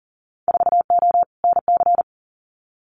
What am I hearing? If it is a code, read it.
Morse code, heard as a keyed tone, about 39 wpm, 700 Hz. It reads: 4O NC
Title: Winter Field Day Exchanges Encore 40wpm